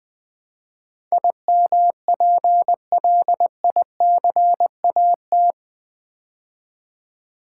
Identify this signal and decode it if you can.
Morse code, heard as a keyed tone, about 20 wpm, 700 Hz. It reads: IMPLICAT
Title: Build A Word - 4 Letters At A Time - 11-Letter Words - 20wpm